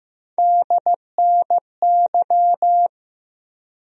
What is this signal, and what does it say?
Morse code, heard as a keyed tone, about 15 wpm, 700 Hz. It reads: DNY
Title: USA Call Sign Postfixes - 3 Letters 15wpm